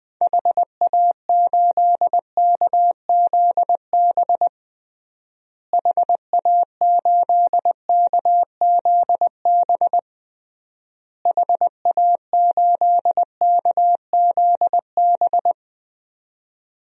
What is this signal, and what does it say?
Morse code, heard as a keyed tone, about 20 wpm, 700 Hz. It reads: HA8KZB HA8KZB HA8KZB
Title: DXCC-100 Call Signs - 6 Characters - Repeated 3x - 20wpm